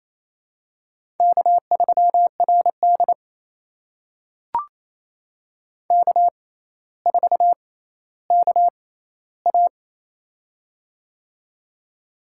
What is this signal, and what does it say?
Morse code, heard as a keyed tone, about 28 wpm, spaced as for 14 wpm, 700 Hz. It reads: K3RD K4KA
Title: USA Call Signs - 4 Characters - 6x Character Spacing - 28wpm